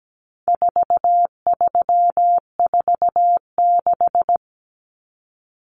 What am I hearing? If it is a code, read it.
Morse code, heard as a keyed tone, about 17 wpm, 700 Hz. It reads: 4346
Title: Numbers - 4-Digit 17wpm